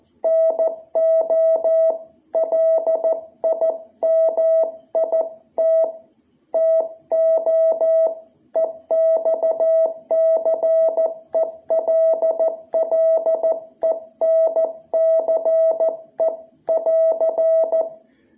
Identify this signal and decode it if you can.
Morse code, heard as a keyed tone, about 14 wpm, 660 Hz. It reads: NO LIMIT TO EXCELLENCE <AR>
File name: jh8cbh / 2021cw